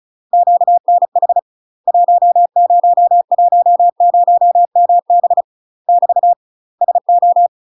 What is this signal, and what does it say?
Morse code, heard as a keyed tone, about 36 wpm, 700 Hz. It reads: QNH 1010MB <BT> SO
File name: UK MASTER QSOs TRAINING / 35WPM / UK MASTER QSOs 0023